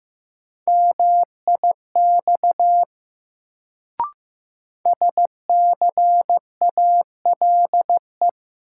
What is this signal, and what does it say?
Morse code, heard as a keyed tone, about 16 wpm, 700 Hz. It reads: MIX SCALE
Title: Sentences from Top 1000 Words - Review Part 4 - New Word Drill - 15wpm